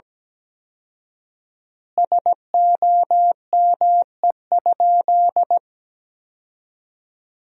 Morse code, heard as a keyed tone, about 17 wpm, 700 Hz. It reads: SOME?
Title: 4x Word-Spacing - Sentences from Top 100 Words - 17wpm